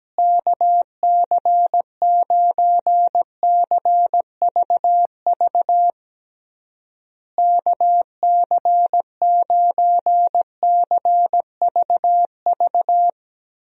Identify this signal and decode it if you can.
Morse code, heard as a keyed tone, about 17 wpm, 700 Hz. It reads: KC9CVV KC9CVV
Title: USA Call Signs - 6 Characters - Repeated 3x - 17wpm